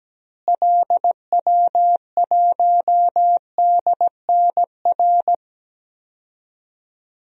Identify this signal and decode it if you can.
Morse code, heard as a keyed tone, about 17 wpm, 700 Hz. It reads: LW1DNR
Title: DXCC-100 Call Signs - 6 Characters - Repeated 2x - 17wpm